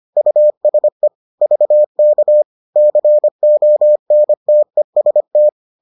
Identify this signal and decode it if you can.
Morse code, heard as a keyed tone, about 25 wpm, 600 Hz. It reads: USE VK CONTEST